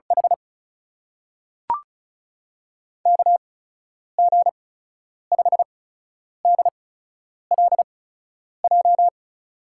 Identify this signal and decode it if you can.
Morse code, heard as a keyed tone, about 36 wpm, spaced as for 10 wpm, 700 Hz. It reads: H KG5DLJ
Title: USA Call Signs - 8x Character Spacing - 35wpm